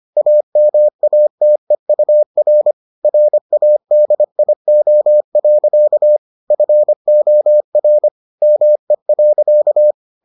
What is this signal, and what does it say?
Morse code, heard as a keyed tone, about 25 wpm, 600 Hz. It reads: AMATEUR RADIO. FOR ME.